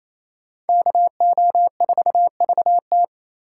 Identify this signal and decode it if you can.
Morse code, heard as a keyed tone, about 35 wpm, 700 Hz. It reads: KO4VT
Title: Speed Racing - USA 2x2 Call Signs 28wpm